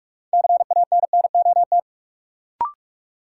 Morse code, heard as a keyed tone, about 46 wpm, 700 Hz. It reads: CANNOT
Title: Sentences from Top 1000 Words - Review Part 6 - New Word Drill - 45wpm